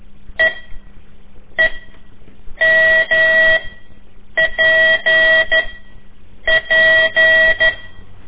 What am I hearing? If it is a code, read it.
Morse code, heard as a keyed tone, about 12 wpm, spaced as for 7 wpm, 1950 Hz. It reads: EEMPP
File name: ik1weg2 / CORSI CW / lez4-3